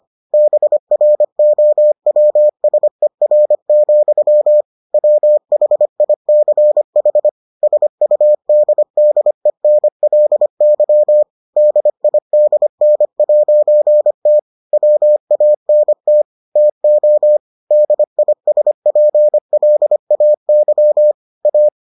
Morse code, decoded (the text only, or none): BROWSER, WHICH SUDDENLY DIDN'T WANT TO DISPLAY A